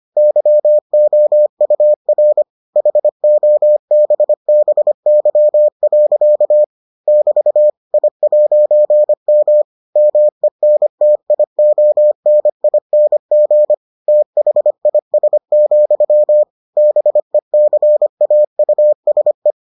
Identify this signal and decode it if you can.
Morse code, heard as a keyed tone, about 25 wpm, 600 Hz. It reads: YOUR HOBBY. <BT> I'M MENTIONING THIS, BECAUSE